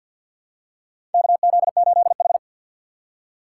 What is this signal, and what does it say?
Morse code, heard as a keyed tone, about 53 wpm, 700 Hz. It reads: KZ8H
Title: USA Call Signs - 6x Character Spacing - 50wpm